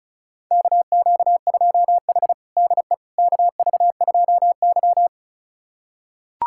Morse code, heard as a keyed tone, about 36 wpm, 700 Hz. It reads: KQ2H DE KV2Y
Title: USA Callsign DE Callsign - 4 Characters - 35wpm